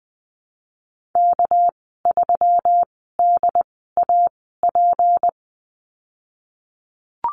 Morse code, heard as a keyed tone, about 19 wpm, spaced as for 14 wpm, 700 Hz. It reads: K3DAP
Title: 200 Lesson 32g - Letter k - Call Signs 20wpm